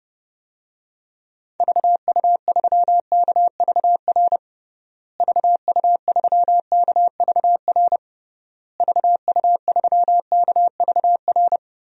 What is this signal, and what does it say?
Morse code, heard as a keyed tone, about 30 wpm, 700 Hz. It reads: VU3KVR VU3KVR VU3KVR
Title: DXCC-100 Call Signs - 6 Characters - Repeated 3x - 30wpm